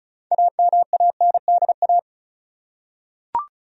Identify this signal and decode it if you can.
Morse code, heard as a keyed tone, about 36 wpm, 700 Hz. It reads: AMANDA
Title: USA Common Names - 35wpm